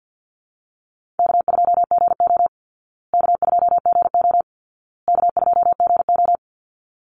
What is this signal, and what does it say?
Morse code, heard as a keyed tone, about 50 wpm, 700 Hz. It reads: K2GO K2GO K2GO
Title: Call Signs Repeated 2x - 50wpm